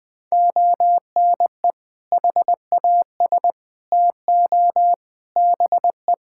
Morse code, heard as a keyed tone, about 21 wpm, 700 Hz. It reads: ONE HAS TO BE
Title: Sentences from Top 1600 Words 20wpm